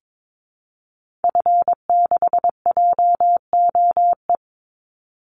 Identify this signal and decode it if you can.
Morse code, heard as a keyed tone, about 22 wpm, 700 Hz. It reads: F6JOE
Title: Call Signs Repeated 3x - 22wpm